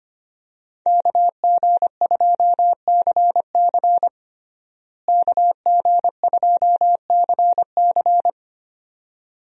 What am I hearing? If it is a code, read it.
Morse code, heard as a keyed tone, about 25 wpm, 700 Hz. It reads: KG2CC KG2CC
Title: USA Call Signs - 5 Characters - Repeated 2x - 25wpm